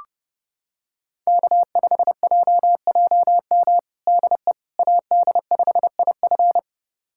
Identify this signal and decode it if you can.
Morse code, heard as a keyed tone, about 30 wpm, 700 Hz. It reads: K5JJM DE AD5IF
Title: USA Callsign DE Callsign - 5 Characters - 30wpm